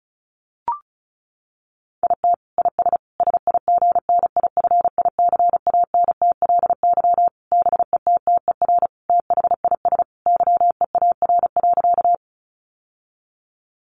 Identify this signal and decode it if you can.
Morse code, heard as a keyed tone, about 34 wpm, 700 Hz. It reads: IT IS SIGNIFICANTLY BETTER THIS YEAR.
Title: Sentences from Top 1500 Words 35wpm